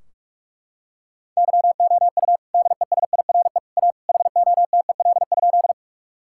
Morse code, heard as a keyed tone, about 41 wpm, 700 Hz. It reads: YOU DESIRE A HOTEL?
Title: Sentences from Top 1600 Words 45wpm